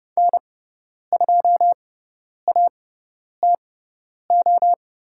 Morse code, heard as a keyed tone, about 30 wpm, spaced as for 10 wpm, 690 Hz. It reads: N2ATO
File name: Lesson 17g - Letter u - Call Signs 30wpm